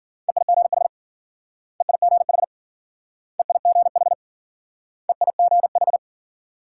Morse code, heard as a keyed tone, about 51 wpm, 700 Hz. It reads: EIGH EIGH EIGH EIGH